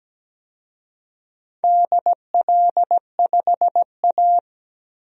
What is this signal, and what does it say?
Morse code, heard as a keyed tone, about 17 wpm, 700 Hz. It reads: DL5A
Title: DXCC-100 Call Signs - 4 Characters - 17wpm